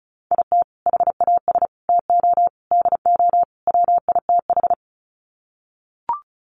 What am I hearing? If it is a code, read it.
Morse code, heard as a keyed tone, about 34 wpm, 700 Hz. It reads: IT HAS TO DO WITH
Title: Sets of 5 Words 35wpm